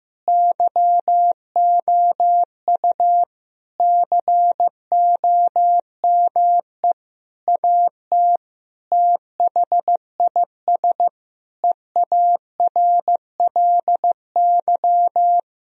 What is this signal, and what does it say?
Morse code, heard as a keyed tone, about 15 wpm, 700 Hz. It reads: YOU COME AT THIS EARLY
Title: Sentences from Top 1000 Words - Review Part 2 - 15wpm